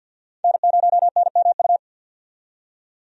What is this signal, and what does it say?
Morse code, heard as a keyed tone, about 52 wpm, 700 Hz. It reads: N0NMU
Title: USA Call Signs - 5 Characters - 50wpm